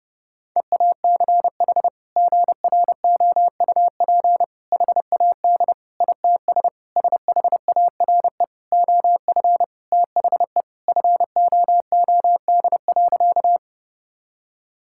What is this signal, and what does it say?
Morse code, heard as a keyed tone, about 32 wpm, 700 Hz. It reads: EACH GROUP HAD ITS SHARE OF THE FOOD.
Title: Sentences from Top 1000 Words - Review Part 6 - 30wpm